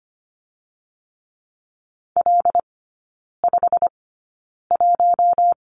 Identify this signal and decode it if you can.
Morse code, heard as a keyed tone, about 25 wpm, spaced as for 9 wpm, 700 Hz. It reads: L51
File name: Lesson 15e - Number 5 - Three Characters 25wpm